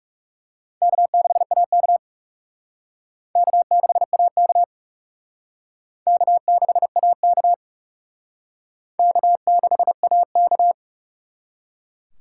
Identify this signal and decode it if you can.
Morse code, heard as a keyed tone, about 37 wpm, 700 Hz. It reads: K6AK K6AK K6AK K6AK
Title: Speed Racing - USA 1x2 Call Signs 30wpm